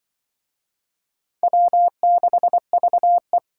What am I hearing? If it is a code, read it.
Morse code, heard as a keyed tone, about 24 wpm, 700 Hz. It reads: W6VE
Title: Speed Racing - USA 1x2 Call Signs 20wpm